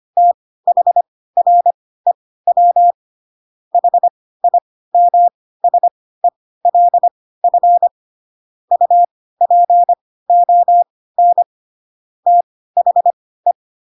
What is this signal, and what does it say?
Morse code, heard as a keyed tone, about 25 wpm, spaced as for 16 wpm, 700 Hz. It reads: THREW HIMSELF UPON THE
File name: Pinocchio / Pinocchio ch2 10 25 ik1hge